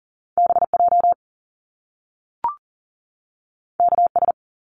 Morse code, heard as a keyed tone, about 39 wpm, 700 Hz. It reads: BJ KS
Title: Top Two Letter Combinations in Words 40wpm